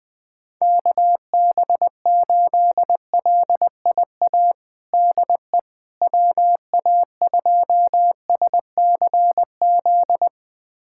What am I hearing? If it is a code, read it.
Morse code, heard as a keyed tone, about 20 wpm, 700 Hz. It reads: KB8LIA DE WA2SCZ
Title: USA Callsign DE Callsign - 6 Characters - 20wpm